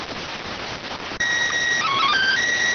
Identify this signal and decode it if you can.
SSTV (slow-scan television), PD 290.